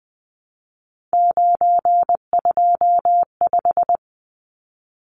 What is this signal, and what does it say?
Morse code, heard as a keyed tone, about 20 wpm, 700 Hz. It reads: 925
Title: Numbers - 3-Digit 20wpm